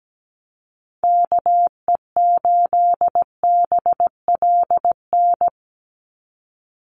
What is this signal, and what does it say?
Morse code, heard as a keyed tone, about 17 wpm, 700 Hz. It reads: KE8BLN
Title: USA Call Signs 17wpm